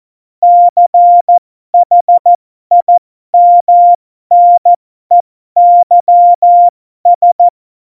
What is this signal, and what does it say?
Morse code, heard as a keyed tone, about 14 wpm, spaced as for 12 wpm, 700 Hz. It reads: CHIMNEYS
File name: Morsekurs / Witze